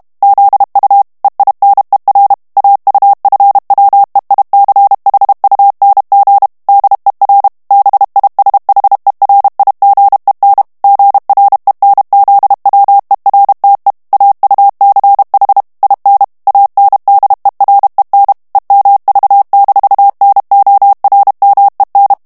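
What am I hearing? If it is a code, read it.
Morse code, heard as a keyed tone, about 32 wpm, 800 Hz. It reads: ZU EINER AUFWEICHUNG DER BISHERIGEN GRENZWERTE AUCH IN ANDEREN EMV-NORMEN
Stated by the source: German plain-text Morse reading practice at 30 words per minute.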